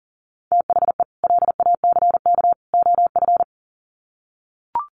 Morse code, heard as a keyed tone, about 40 wpm, 700 Hz. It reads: THE LACK OF
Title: Sets of 3 Words 40wpm